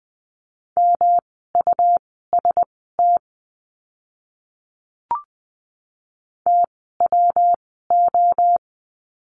Morse code, heard as a keyed tone, about 20 wpm, spaced as for 14 wpm, 700 Hz. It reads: MUST TWO
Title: 237 Lesson 38f - Letter q - Words 20wpm